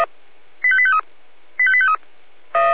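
Five-tone selective call (ZVEI), heard as quick runs of five tones, tones of about 70 ms. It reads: 85853 85853
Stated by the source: ZVEI five-tone selective call received on SDR.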